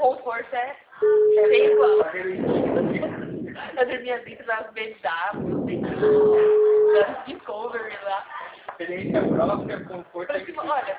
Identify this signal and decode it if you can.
Ringback tone (Europe, 425 Hz), with speech.